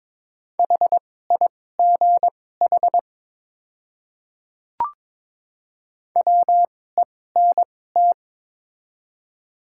Morse code, heard as a keyed tone, about 22 wpm, spaced as for 16 wpm, 700 Hz. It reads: HIGH WENT
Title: 2x Character Spacing - Top 200 Words - 22wpm